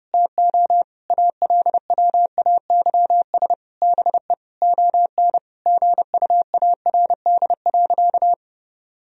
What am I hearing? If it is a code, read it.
Morse code, heard as a keyed tone, about 32 wpm, 700 Hz. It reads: TO ALWAYS BE ON GUARD.
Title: Sentences from Top 1000 Words - Review Part 4 - 30wpm